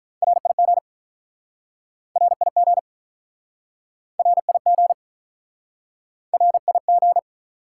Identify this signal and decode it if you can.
Morse code, heard as a keyed tone, about 47 wpm, 700 Hz. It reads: RIG RIG RIG RIG